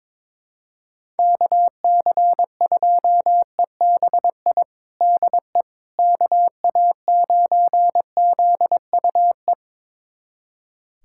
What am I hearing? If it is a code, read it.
Morse code, heard as a keyed tone, about 22 wpm, 700 Hz. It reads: KC2EBI DE KA9ZUE
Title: USA Callsign DE Callsign 22wpm